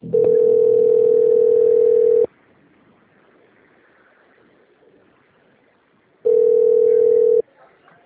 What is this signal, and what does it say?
Ringback tone (North America), 440 + 480 Hz.